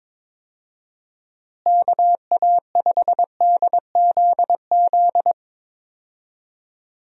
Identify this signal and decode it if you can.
Morse code, heard as a keyed tone, about 22 wpm, 700 Hz. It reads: KA5DZZ
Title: USA Call Signs - 6 Characters - Repeated 2x - 22wpm